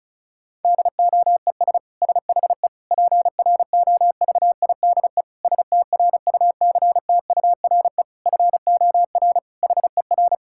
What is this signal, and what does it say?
Morse code, heard as a keyed tone, about 36 wpm, 700 Hz. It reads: DOES SHE PROVIDE STRUCTURE FOR HER
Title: Sentences from Top 1000 Words - Review Part 6 - 35wpm